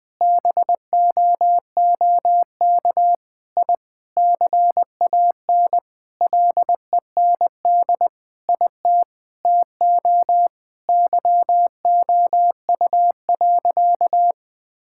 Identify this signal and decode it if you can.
Morse code, heard as a keyed tone, about 21 wpm, 700 Hz. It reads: BOOK I CAN LEND IT TO YOU.
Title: Sentences with 5-Word Phrases - Spoken Beginning - 20wpm